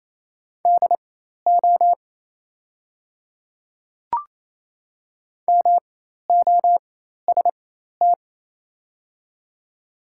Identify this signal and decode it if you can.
Morse code, heard as a keyed tone, about 28 wpm, spaced as for 13 wpm, 690 Hz. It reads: DO MOST
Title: 4x Character Spacing - Top 100 Words - 28wpm